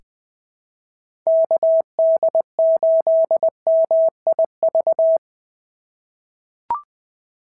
Morse code, heard as a keyed tone, about 21 wpm, 690 Hz. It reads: KD8MIV
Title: POTA QSO Practice - 20wpm